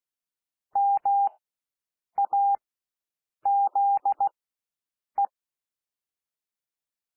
Morse code, heard as a keyed tone, about 16 wpm, spaced as for 7 wpm, 800 Hz. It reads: MAZE
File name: SOUND WORDS 16 wpm